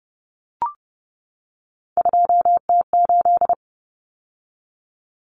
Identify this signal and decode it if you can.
Morse code, heard as a keyed tone, about 30 wpm, 700 Hz. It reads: 2T8
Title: Common Cut Numbers - 3-Digit 30wpm